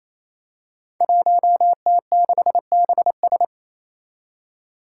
Morse code, heard as a keyed tone, about 28 wpm, 700 Hz. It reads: 1T6BS
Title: Five Challenging Letter-Number Combinations 28wpm